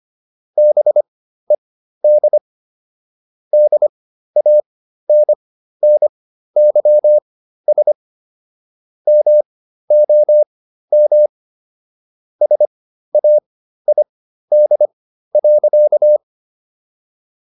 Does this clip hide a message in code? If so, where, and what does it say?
Yes: Morse code from 0.6 to 16.2 s, heard as a keyed tone, about 25 wpm, spaced as for 13 wpm, 600 Hz. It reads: BED DANNYS MOM SAID.